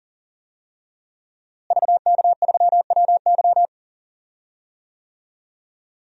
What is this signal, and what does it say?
Morse code, heard as a keyed tone, about 37 wpm, 700 Hz. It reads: VK3WY
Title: DXCC-100 Call Signs - 5 Characters - Repeated 2x - 40wpm